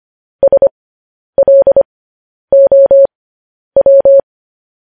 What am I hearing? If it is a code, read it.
Morse code, heard as a keyed tone, about 25 wpm, spaced as for 10 wpm, 550 Hz. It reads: SLOW
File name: podcast / News headlines-1x-25x10